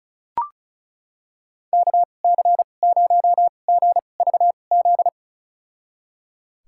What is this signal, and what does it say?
Morse code, heard as a keyed tone, about 36 wpm, spaced as for 25 wpm, 700 Hz. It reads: KC0GVZ
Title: USA Call Signs - 2x Character Spacing - 35wpm